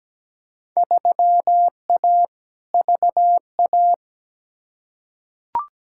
Morse code, heard as a keyed tone, about 17 wpm, 700 Hz. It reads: 3A VA
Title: Field Day Exchanges - 17wpm